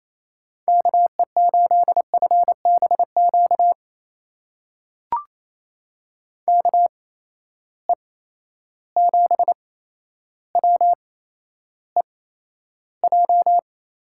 Morse code, heard as a keyed tone, about 28 wpm, spaced as for 12 wpm, 700 Hz. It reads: KE8FBQ KE7WEJ
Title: USA Call Signs - 6 Characters - 8x Character Spacing - 28wpm